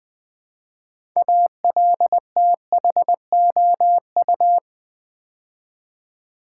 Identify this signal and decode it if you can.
Morse code, heard as a keyed tone, about 20 wpm, 700 Hz. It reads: ALTHOU
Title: Build A Word - 3 Letters At A Time - 8-Letter Words - 20wpm